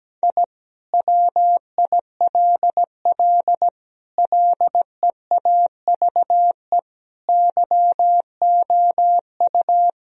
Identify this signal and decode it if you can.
Morse code, heard as a keyed tone, about 17 wpm, 700 Hz. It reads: I WILL LEAVE YOU
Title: Sentences from Top 1000 Words - Review Part 4 - 17wpm